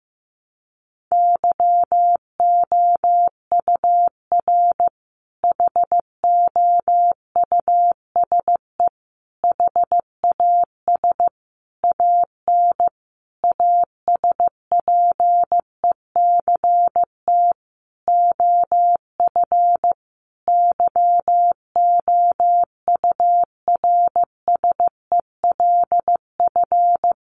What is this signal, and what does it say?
Morse code, heard as a keyed tone, about 15 wpm, 700 Hz. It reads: YOUR HOUSE HAS AN ASPECT OF YOURSELF
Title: Sentences from Top 1300 Words 15wpm